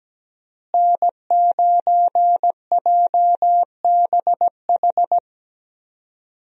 Morse code, heard as a keyed tone, about 17 wpm, 700 Hz. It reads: N9JBH
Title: USA Call Signs - 5 Characters - Repeated 2x - 17wpm